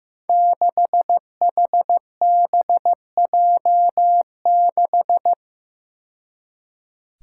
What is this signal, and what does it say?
Morse code, heard as a keyed tone, about 15 wpm, 700 Hz. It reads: 6HBJ6
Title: Five Challenging Letter-Number Combinations 15wpm